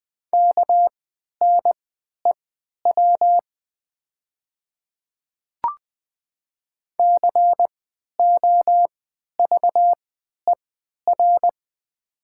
Morse code, heard as a keyed tone, about 20 wpm, spaced as for 11 wpm, 700 Hz. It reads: KNEW COVER
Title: 3x Character Spacing - Top 500 Words - 20wpm